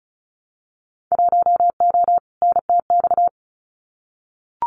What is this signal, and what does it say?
Morse code, heard as a keyed tone, about 34 wpm, 700 Hz. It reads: 1O NTX
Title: Winter Field Day Exchanges 35wpm